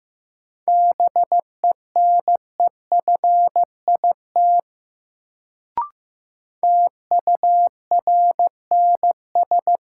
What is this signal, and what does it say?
Morse code, heard as a keyed tone, about 15 wpm, 700 Hz. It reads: BENEFIT TURNS
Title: Sets of 5 Words Encore - New Word Drill - 15wpm